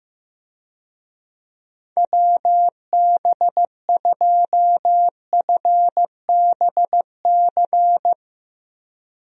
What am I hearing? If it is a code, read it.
Morse code, heard as a keyed tone, about 15 wpm, 700 Hz. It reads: WB2FBC
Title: USA Call Signs - 6 Characters - Repeated 3x - 15wpm